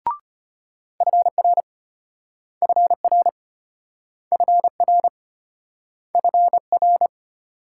Morse code, heard as a keyed tone, about 35 wpm, 700 Hz. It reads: FR FR FR FR